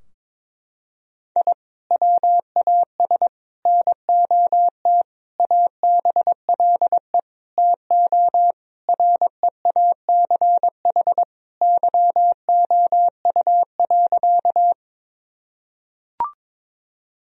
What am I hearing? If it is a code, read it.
Morse code, heard as a keyed tone, about 22 wpm, 700 Hz. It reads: I WAS NOT ABLE TO REACH YOU.
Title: Sentences from Top 1000 Words - Review Part 2 - 22wpm